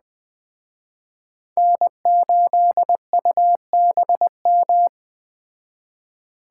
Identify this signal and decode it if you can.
Morse code, heard as a keyed tone, about 21 wpm, 700 Hz. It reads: N8UBM
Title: Speed Racing - USA 1x3 Call Signs 17wpm